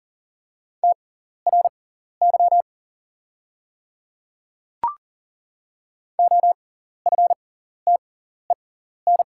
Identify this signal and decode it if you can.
Morse code, heard as a keyed tone, about 40 wpm, spaced as for 14 wpm, 690 Hz. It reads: TRY OFTEN
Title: 6x Character Spacing - Top 300 Words - 40wpm